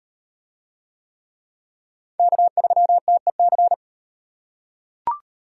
Morse code, heard as a keyed tone, about 38 wpm, 700 Hz. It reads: K3TEC